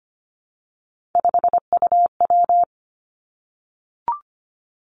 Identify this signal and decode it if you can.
Morse code, heard as a keyed tone, about 25 wpm, 700 Hz. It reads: 5UW
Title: Three Challenging Letter-Number Combinations 25wpm